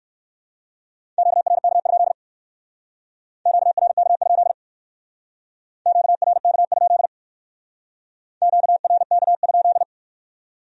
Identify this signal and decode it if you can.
Morse code, heard as a keyed tone, about 54 wpm, 700 Hz. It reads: QRK? QRK? QRK? QRK?
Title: Speed-Racing Q-Codes - 45wpm